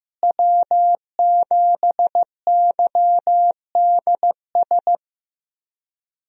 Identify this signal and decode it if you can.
Morse code, heard as a keyed tone, about 15 wpm, 700 Hz. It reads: W7YDS
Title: USA Call Signs - 5 Characters - Repeated 3x - 15wpm